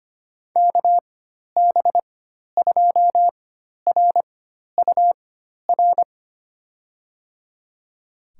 Morse code, heard as a keyed tone, about 25 wpm, spaced as for 12 wpm, 700 Hz. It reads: KB2RUR
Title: USA Call Signs - 4x Character Spacing - 25wpm